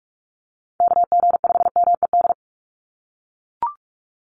Morse code, heard as a keyed tone, about 44 wpm, 700 Hz. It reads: KG5MED
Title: USA Call Signs 45wpm